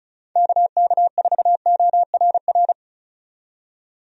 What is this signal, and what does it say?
Morse code, heard as a keyed tone, about 35 wpm, 700 Hz. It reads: KK4ORR
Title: USA Call Signs - 6 Characters - 35wpm